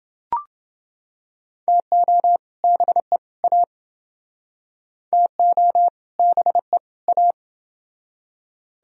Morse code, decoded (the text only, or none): TO BE A TO BE A